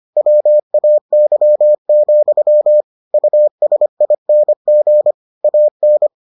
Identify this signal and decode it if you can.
Morse code, heard as a keyed tone, about 25 wpm, 600 Hz. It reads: WAY, USING AN